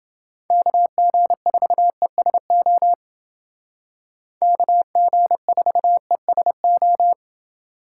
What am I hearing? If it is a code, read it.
Morse code, heard as a keyed tone, about 29 wpm, 700 Hz. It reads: KG4ESO KG4ESO